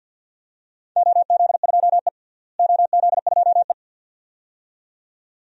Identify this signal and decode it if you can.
Morse code, heard as a keyed tone, about 52 wpm, 700 Hz. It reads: OZ2E OZ2E
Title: DXCC-100 Call Signs - 4 Characters - Repeated 2x - 50wpm